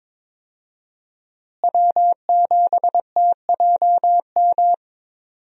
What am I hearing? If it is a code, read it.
Morse code, heard as a keyed tone, about 21 wpm, 700 Hz. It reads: W7TJM
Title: USA Call Signs - 5 Characters - Repeated 3x - 22wpm